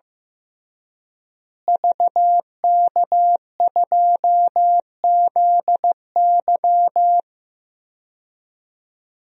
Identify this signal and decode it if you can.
Morse code, heard as a keyed tone, about 16 wpm, 700 Hz. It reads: VK2ZY
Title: DXCC-100 Call Signs - 5 Characters - Repeated 2x - 15wpm